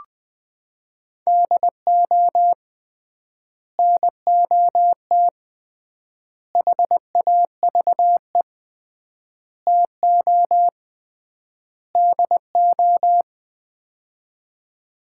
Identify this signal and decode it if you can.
Morse code, heard as a keyed tone, about 20 wpm, 700 Hz. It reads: DO NOT HAVE TO DO
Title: 3x Word-Spacing - Sets of 5 Words - 20wpm